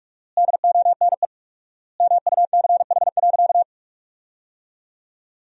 Morse code, heard as a keyed tone, about 47 wpm, 700 Hz. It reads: DONE MUCH.
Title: 4x Word-Spacing - Sentences from Top 500 Words - 45wpm